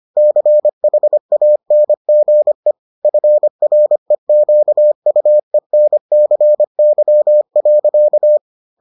Morse code, heard as a keyed tone, about 25 wpm, 600 Hz. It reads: CHANGE FREQUENCY.